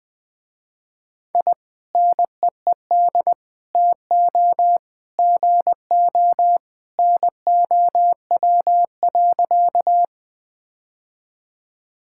Morse code, heard as a keyed tone, about 21 wpm, 700 Hz. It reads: I NEED TO GO NOW.
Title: Sentences with 2-Word Phrases - Spoken Beginning - 20wpm